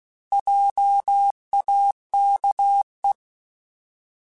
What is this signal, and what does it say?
Morse code, heard as a keyed tone, about 16 wpm, 800 Hz. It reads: JAKE
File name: Alphabetical / Alphabetical mp3s / Js @15